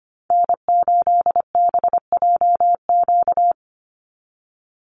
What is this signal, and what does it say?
Morse code, heard as a keyed tone, about 25 wpm, 700 Hz. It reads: N8BJQ
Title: Call Signs - 25wpm